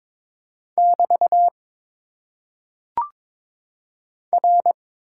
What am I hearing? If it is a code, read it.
Morse code, heard as a keyed tone, about 22 wpm, 700 Hz. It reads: <BT> R
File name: Top 90 QSO Elements - 6x Character Spacing - 22wpm